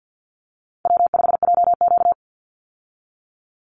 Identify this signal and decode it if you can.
Morse code, heard as a keyed tone, about 50 wpm, 700 Hz. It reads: W5JQ
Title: Call Signs Repeated 2x - 50wpm